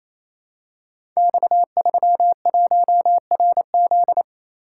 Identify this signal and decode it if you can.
Morse code, heard as a keyed tone, about 27 wpm, 700 Hz. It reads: X31RZ
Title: Code Groups - 5 Letter:Numbers - 28wpm